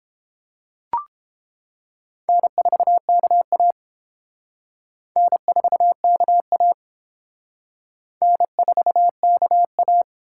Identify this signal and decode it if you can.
Morse code, heard as a keyed tone, about 27 wpm, 700 Hz. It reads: N4KA N4KA N4KA
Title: Speed Racing - USA Call Signs 22wpm